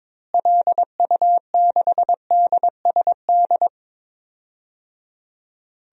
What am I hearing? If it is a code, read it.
Morse code, heard as a keyed tone, about 22 wpm, 700 Hz. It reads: LU6DSD
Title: DXCC-100 Call Signs - 6 Characters - Repeated 3x - 22wpm